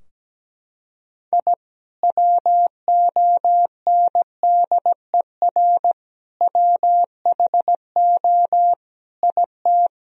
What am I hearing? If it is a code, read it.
Morse code, heard as a keyed tone, about 17 wpm, 700 Hz. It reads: I WONDER WHO IT
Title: Sentences from Top 1000 Words - Review Part 3 - 17wpm